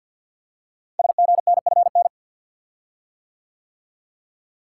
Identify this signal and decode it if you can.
Morse code, heard as a keyed tone, about 52 wpm, 700 Hz. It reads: SGNFN
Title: Code Groups - 5 Letters - 50wpm